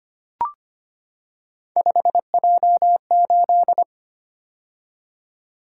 Morse code, heard as a keyed tone, about 25 wpm, 700 Hz. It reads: 5J8
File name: Three Letter-Number Combinations - 25wpm